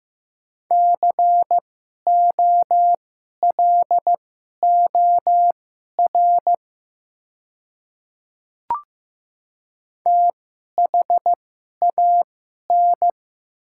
Morse code, heard as a keyed tone, about 15 wpm, spaced as for 11 wpm, 700 Hz. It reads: COLOR THAN
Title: 2x Character Spacing - Top 300 Words - 15wpm